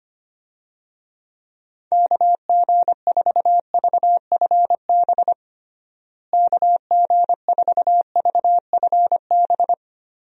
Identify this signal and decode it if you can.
Morse code, heard as a keyed tone, about 25 wpm, 700 Hz. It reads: KG4VFB KG4VFB
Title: USA Call Signs - 6 Characters - Repeated 2x - 25wpm